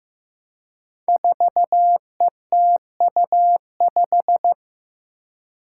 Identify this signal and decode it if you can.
Morse code, heard as a keyed tone, about 15 wpm, 700 Hz. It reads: 4ETU5
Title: Five Challenging Letter-Number Combinations 15wpm